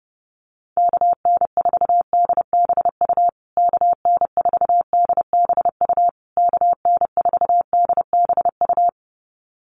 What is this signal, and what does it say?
Morse code, heard as a keyed tone, about 28 wpm, 700 Hz. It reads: KN4DBU KN4DBU KN4DBU
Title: USA Call Signs Repeated 3x 30wpm